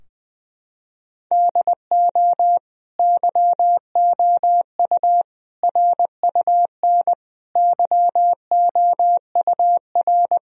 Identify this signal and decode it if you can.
Morse code, heard as a keyed tone, about 20 wpm, 700 Hz. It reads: DO YOU RUN YOUR
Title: Sentences from Top 1000 Words - Review Part 4 - 20wpm